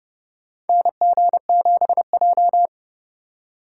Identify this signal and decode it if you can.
Morse code, heard as a keyed tone, about 32 wpm, 700 Hz. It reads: NG7J
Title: USA Call Signs - 2x Character Spacing - 30wpm